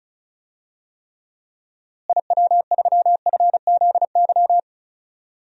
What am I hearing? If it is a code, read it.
Morse code, heard as a keyed tone, about 34 wpm, 700 Hz. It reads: IW3FZY
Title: DXCC-100 Call Signs - 6 Characters - Repeated 3x - 35wpm